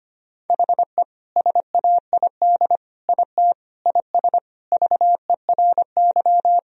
Morse code, heard as a keyed tone, about 25 wpm, 700 Hz. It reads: HE SAID IT IS VERY
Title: Sentences from Top 1000 Words - Review Part 2 - 25wpm